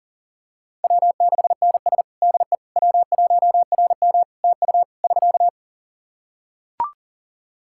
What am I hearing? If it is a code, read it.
Morse code, heard as a keyed tone, about 40 wpm, 700 Hz. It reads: W6NS DE W1RM TU <SK>
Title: Practice QSO - Exchange 4 - Ending - 40wpm